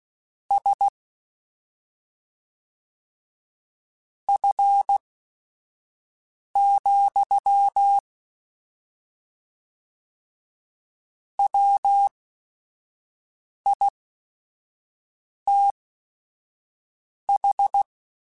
Morse code, heard as a keyed tone, about 16 wpm, spaced as for 5 wpm, 800 Hz. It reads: S F, WITH
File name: Weather Sets / Weather Sets-mp3s / Weather 4 24 @5